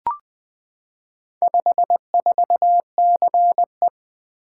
Morse code, heard as a keyed tone, about 20 wpm, 700 Hz. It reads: 54CE
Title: Four Letter-Number Combinations - 20wpm